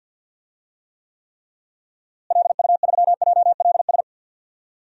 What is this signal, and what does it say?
Morse code, heard as a keyed tone, about 47 wpm, 700 Hz. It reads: LU3JLS
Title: DXCC-100 Call Signs - 6 Characters - Repeated 3x - 50wpm